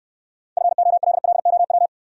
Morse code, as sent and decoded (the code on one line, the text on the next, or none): ..-. --- .-.. .-.. --- .--
FOLLOW